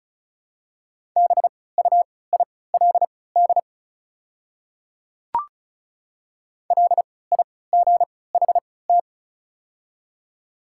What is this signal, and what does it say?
Morse code, heard as a keyed tone, about 36 wpm, spaced as for 20 wpm, 690 Hz. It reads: BUILD LIGHT
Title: 3x Character Spacing - Top 200 Words - 35wpm